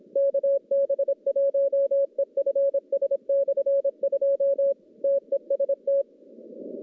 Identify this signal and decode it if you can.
Morse code, heard as a keyed tone, about 26 wpm, 550 Hz. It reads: KB1EFS/2 TEST
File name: contests / CW-SS-k5tr- -1100